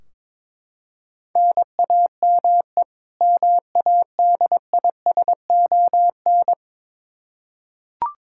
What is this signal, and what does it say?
Morse code, heard as a keyed tone, about 21 wpm, 700 Hz. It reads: NAME MADISON
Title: Practice QSO - Exchange 1 - Beginning - 22wpm